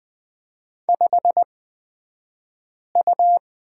Morse code, heard as a keyed tone, about 20 wpm, spaced as for 5 wpm, 700 Hz. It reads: 5U
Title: 228 Lesson 37d - Letter x - Two Characters 20wpm